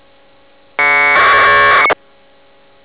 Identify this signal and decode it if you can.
Packet radio (AX.25, as in APRS).